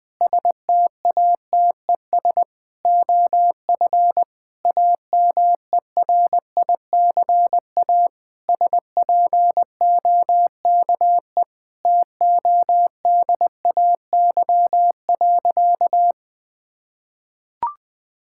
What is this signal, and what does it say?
Morse code, heard as a keyed tone, about 20 wpm, 700 Hz. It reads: STATES OF AMERICA SPOKE TODAY.
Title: Sentences with 8-Word Phrases - Spoken Beginning - 20wpm